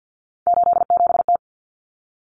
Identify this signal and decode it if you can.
Morse code, heard as a keyed tone, about 50 wpm, 700 Hz. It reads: 87T
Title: Common Cut Numbers - 3-Digit 50wpm